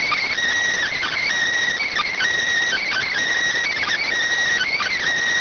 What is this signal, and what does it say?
Emergency Alert System (SAME) tones, heard as harsh buzzing bursts. It is heard from 0.9 to 5.0 s.